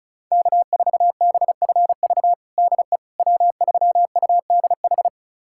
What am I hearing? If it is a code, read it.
Morse code, heard as a keyed tone, about 36 wpm, 690 Hz. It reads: K4BFV DE W3UDH
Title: USA Callsign DE Callsign - 5 Characters - 35wpm